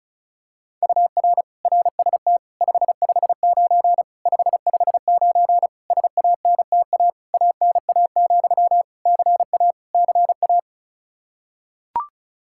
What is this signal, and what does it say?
Morse code, heard as a keyed tone, about 36 wpm, 700 Hz. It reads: UR RST 559 559 SANTA ANA, CA CA
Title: Practice QSO - Exchange 1 - Beginning - 35wpm